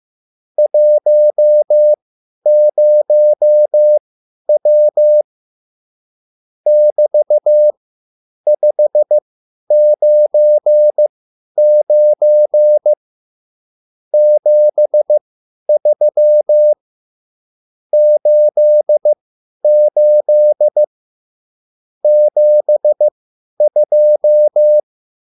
Morse code, heard as a keyed tone, about 15 wpm, spaced as for 10 wpm, 600 Hz. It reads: 10W <BT>599 73 88 72